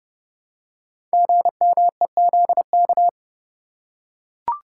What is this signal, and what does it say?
Morse code, heard as a keyed tone, about 30 wpm, 700 Hz. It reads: GMEZK
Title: Code Groups - 5 Letter:Numbers - 30wpm